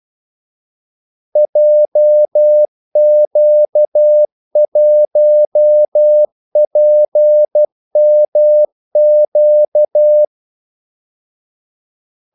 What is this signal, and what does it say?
Morse code, heard as a keyed tone, about 12 wpm, 600 Hz. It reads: JQ1PMQ
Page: 2 ekstra ordmellemrum mellem kaldesignalerne